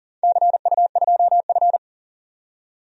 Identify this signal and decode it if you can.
Morse code, heard as a keyed tone, about 37 wpm, 700 Hz. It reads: CU2F
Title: DXCC-100 Call Signs - 4 Characters - 40wpm